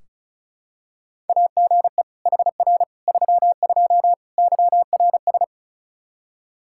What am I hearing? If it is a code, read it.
Morse code, heard as a keyed tone, about 36 wpm, 700 Hz. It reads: AGE HR 32 YRS
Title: Practice QSO - Exchange 3 - More Info - 35wpm